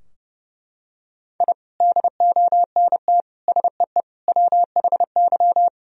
Morse code, heard as a keyed tone, about 30 wpm, 700 Hz. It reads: I DONT SEE WHY
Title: Sentences from Top 1000 Words - Review Part 2 - 30wpm